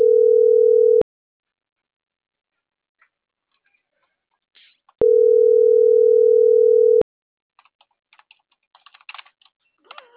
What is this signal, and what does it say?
Ringback tone (North America), 440 + 480 Hz.